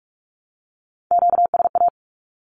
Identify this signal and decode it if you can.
Morse code, heard as a keyed tone, about 44 wpm, 700 Hz. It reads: QSA
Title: Q-Codes 45wpm